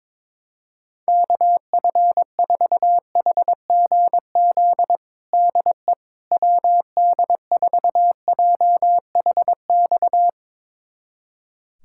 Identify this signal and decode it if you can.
Morse code, heard as a keyed tone, about 21 wpm, 700 Hz. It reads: KF4HGZ DE WD4JHX
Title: USA Callsign DE Callsign - 6 Characters - 22wpm